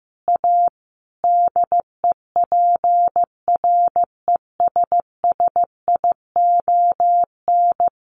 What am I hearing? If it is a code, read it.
Morse code, heard as a keyed tone, about 15 wpm, 700 Hz. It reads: A DEPRESSION
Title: Sentences from Top 1300 Words 15wpm